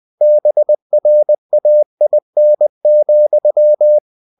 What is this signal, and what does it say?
Morse code, heard as a keyed tone, about 20 wpm, 600 Hz. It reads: BRAIN,
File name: cw-books / winnie-the-pooh / winnie-20wpm-page-0091